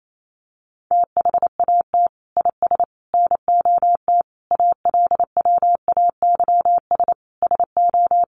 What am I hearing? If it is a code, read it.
Morse code, heard as a keyed tone, about 28 wpm, 700 Hz. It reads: THAT IS NOT ALWAYS SO
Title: Sentences from Top 700 Words 28wpm